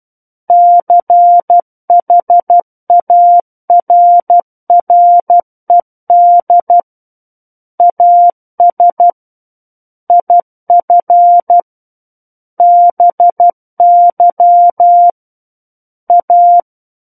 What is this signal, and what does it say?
Morse code, heard as a keyed tone, about 12 wpm, 700 Hz. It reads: CHARRED AS IF BY A
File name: Book One / 03-Chapter 3 - 12 WPM